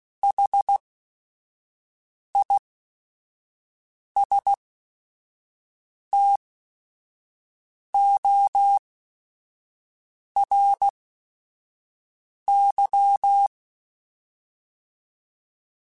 Morse code, heard as a keyed tone, about 16 wpm, spaced as for 5 wpm, 800 Hz. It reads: HISTORY
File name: T then H Runs / T then H Runs.mp3s / T then H Runs @5